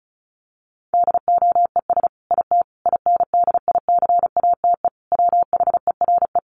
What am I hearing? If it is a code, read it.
Morse code, heard as a keyed tone, about 34 wpm, 700 Hz. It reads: DOES IT INDICATE WHERE
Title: Sentences from Top 1300 Words 35wpm